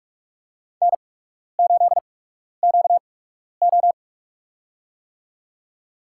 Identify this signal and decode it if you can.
Morse code, heard as a keyed tone, about 46 wpm, spaced as for 12 wpm, 700 Hz. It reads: N8QO
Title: USA Call Signs - 4 Characters - 8x Character Spacing - 45wpm